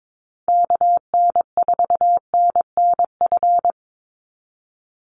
Morse code, heard as a keyed tone, about 22 wpm, 700 Hz. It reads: KN4NNF
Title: USA Call Signs 22wpm